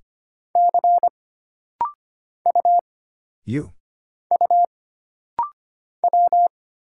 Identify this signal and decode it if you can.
Morse code, heard as a keyed tone, about 25 wpm, 700 Hz. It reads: C U U W
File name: Single Character (Letter:Number:Punctuation:Prosign) - Rapid-Fire - 25wpm